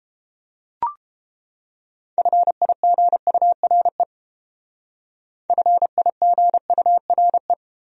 Morse code, heard as a keyed tone, about 31 wpm, 700 Hz. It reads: FIGURE FIGURE